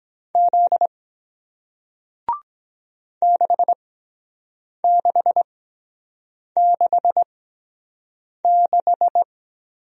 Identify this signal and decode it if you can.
Morse code, heard as a keyed tone, about 23 wpm, spaced as for 7 wpm, 700 Hz. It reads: Z 6666